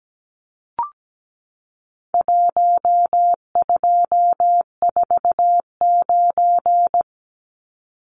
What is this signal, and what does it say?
Morse code, heard as a keyed tone, about 17 wpm, 700 Hz. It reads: 1249
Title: Numbers - 4-Digit 17wpm